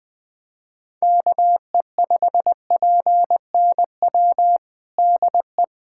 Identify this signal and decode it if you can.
Morse code, heard as a keyed tone, about 21 wpm, 700 Hz. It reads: KE5PNW DE
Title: USA Callsign DE Callsign - 6 Characters - 20wpm